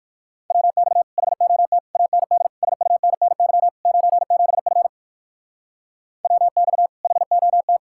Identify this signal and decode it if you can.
Morse code, heard as a keyed tone, about 56 wpm, 700 Hz. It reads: WX HOT AND SUNNY 97F WX HOT